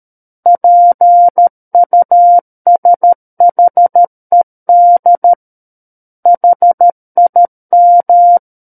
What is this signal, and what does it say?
Morse code, heard as a keyed tone, about 13 wpm, 700 Hz. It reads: PUSHED HIM
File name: Book One / 04-Chapter 4 - 13 WPM